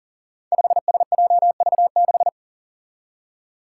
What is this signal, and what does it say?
Morse code, heard as a keyed tone, about 40 wpm, 700 Hz. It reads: 5SJV6
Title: Five Challenging Letter-Number Combinations 40wpm